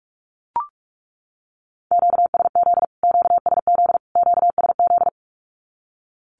Morse code, heard as a keyed tone, about 44 wpm, 700 Hz. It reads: QSZ QSZ QSZ
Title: Q-Codes Repeated 3x 45wpm